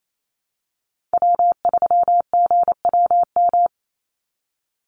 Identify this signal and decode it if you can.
Morse code, heard as a keyed tone, about 28 wpm, 700 Hz. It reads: W3GWM
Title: Call Signs Repeated 2x - 28wpm